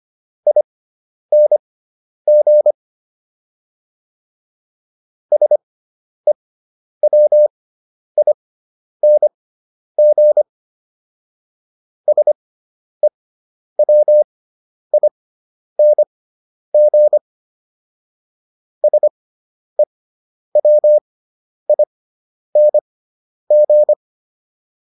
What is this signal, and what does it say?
Morse code, heard as a keyed tone, about 25 wpm, spaced as for 10 wpm, 600 Hz. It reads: ING SEWING SEWING SEWING